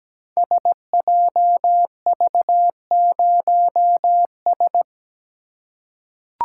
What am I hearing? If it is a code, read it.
Morse code, heard as a keyed tone, about 17 wpm, 700 Hz. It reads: SJV0S
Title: Code Groups - 5 Letter:Numbers - 17wpm